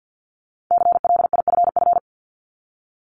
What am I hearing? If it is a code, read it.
Morse code, heard as a keyed tone, about 50 wpm, 700 Hz. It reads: CLIFF
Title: Multi-letter Phonemes Repeated 3x 50wpm